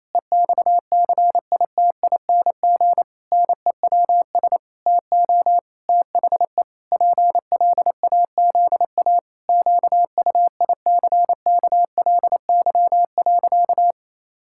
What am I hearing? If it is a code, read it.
Morse code, heard as a keyed tone, about 28 wpm, 700 Hz. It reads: EXCITING NEWS TO THE PLAZA QUICKLY.
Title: Pangram Challenge v2 - 28wpm